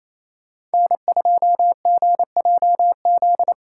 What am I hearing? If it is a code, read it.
Morse code, heard as a keyed tone, about 36 wpm, 700 Hz. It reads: N2GJZ